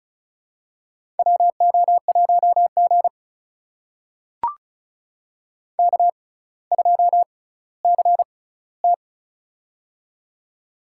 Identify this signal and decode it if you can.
Morse code, heard as a keyed tone, about 34 wpm, spaced as for 18 wpm, 700 Hz. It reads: WO1G K2CT
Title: USA Call Signs - 6x Character Spacing - 35wpm